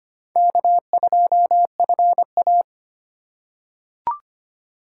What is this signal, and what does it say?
Morse code, heard as a keyed tone, about 25 wpm, 690 Hz. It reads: K2FA
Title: USA Call Signs - 4 Characters - 4x Character Spacing - 25wpm